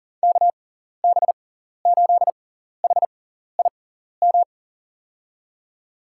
Morse code, heard as a keyed tone, about 39 wpm, spaced as for 14 wpm, 700 Hz. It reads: KB8HIM
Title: USA Call Signs - 6 Characters - 6x Character Spacing - 40wpm